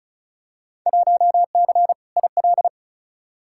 Morse code, heard as a keyed tone, about 34 wpm, 700 Hz. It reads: 1C IL
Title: Field Day Exchanges - 35wpm